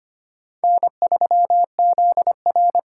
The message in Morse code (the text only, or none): N3ZR